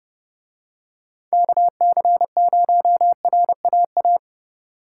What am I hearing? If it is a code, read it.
Morse code, heard as a keyed tone, about 30 wpm, 700 Hz. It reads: KC0RAA